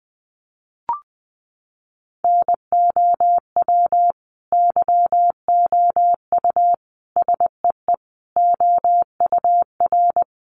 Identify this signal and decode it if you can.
Morse code, heard as a keyed tone, about 20 wpm, 700 Hz. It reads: NOW YOU SEE OUR
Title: Sentences from Top 1500 Words 20wpm